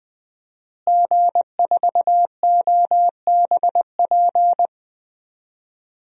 Morse code, heard as a keyed tone, about 20 wpm, 700 Hz. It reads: G4OBP
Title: DXCC-100 Call Signs - 5 Characters - Repeated 2x - 20wpm